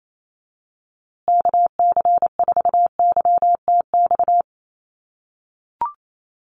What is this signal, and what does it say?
Morse code, heard as a keyed tone, about 28 wpm, 700 Hz. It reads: KC4YTX
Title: USA Call Signs Repeated 3x 28wpm